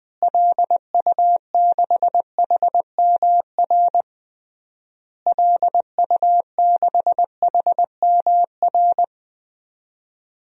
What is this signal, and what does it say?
Morse code, heard as a keyed tone, about 20 wpm, 700 Hz. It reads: LU6HMR LU6HMR
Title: DXCC-100 Call Signs - 6 Characters - Repeated 3x - 20wpm